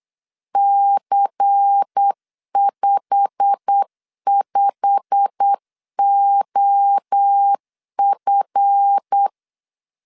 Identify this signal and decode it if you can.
Morse code, heard as a keyed tone, about 8 wpm, 800 Hz. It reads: C55OF
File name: LEZIONI / 8 YJ45 / 8 YJ45 8wpm 3214